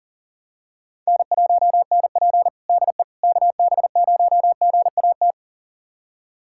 Morse code, heard as a keyed tone, about 40 wpm, 700 Hz. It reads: N1NP DE KB0GAT
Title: USA Callsign DE Callsign 40wpm